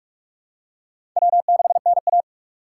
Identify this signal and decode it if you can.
Morse code, heard as a keyed tone, about 46 wpm, 700 Hz. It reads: W6NA
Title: USA Call Signs - 4 Characters - 6x Character Spacing - 45wpm